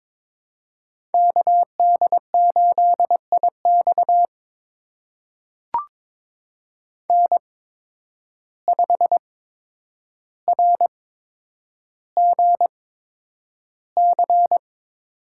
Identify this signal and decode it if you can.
Morse code, heard as a keyed tone, about 22 wpm, spaced as for 9 wpm, 700 Hz. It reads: KD8IX N5RGC
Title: USA Call Signs - 5 Characters - 8x Character Spacing - 22wpm